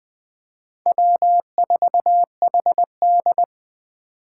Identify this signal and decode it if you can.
Morse code, heard as a keyed tone, about 20 wpm, 700 Hz. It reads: W4HD
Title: Four Challenging Letter-Number Combinations 20wpm